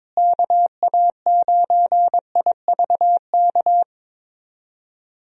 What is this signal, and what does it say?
Morse code, heard as a keyed tone, about 23 wpm, 700 Hz. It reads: KA9IVK
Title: USA Call Signs - 6 Characters - Repeated 2x - 22wpm